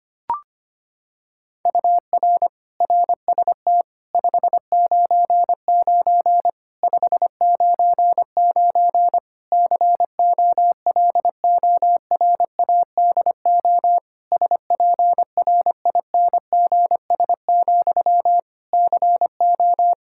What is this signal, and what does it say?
Morse code, heard as a keyed tone, about 25 wpm, 700 Hz. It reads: UR RST 599 599 COLORADO SPRINGS, CO
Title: K1USN SST QSO Practice 25wpm